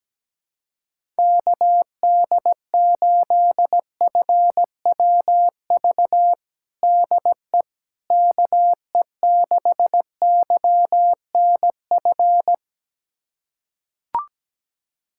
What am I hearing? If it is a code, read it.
Morse code, heard as a keyed tone, about 17 wpm, 700 Hz. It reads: KD8FWV DE KE6YNF
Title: USA Callsign DE Callsign 17wpm